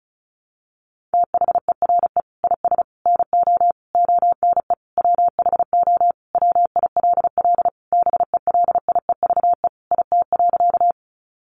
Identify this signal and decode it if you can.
Morse code, heard as a keyed tone, about 34 wpm, 700 Hz. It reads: THERE IS NO ONE WHO WILL BELIEVE IT.
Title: Sentences from Top 1000 Words 35wpm